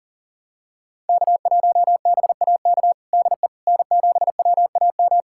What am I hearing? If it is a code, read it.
Morse code, heard as a keyed tone, about 40 wpm, 700 Hz. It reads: K1BAK DE N7WAM
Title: USA Callsign DE Callsign - 5 Characters - 40wpm